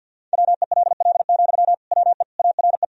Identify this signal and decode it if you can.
Morse code, heard as a keyed tone, about 46 wpm, 700 Hz. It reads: WELL, WE ARE
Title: Sentences from Top 1000 Words - Review Part 4 - 50wpm